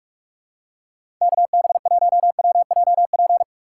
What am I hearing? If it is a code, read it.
Morse code, heard as a keyed tone, about 41 wpm, 700 Hz. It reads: KB1WJP
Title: USA Call Signs - 6 Characters - 4x Character Spacing - 45wpm